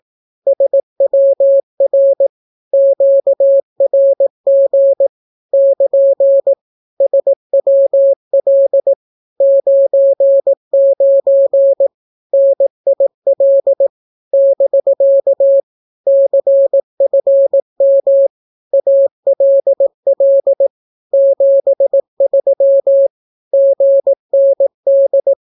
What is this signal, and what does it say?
Morse code, heard as a keyed tone, about 18 wpm, 550 Hz. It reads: SWR QRG <KN> SWL 99 NIL <BK> CFM ALL 73 GND
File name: naukapodstaw / skroty / skroty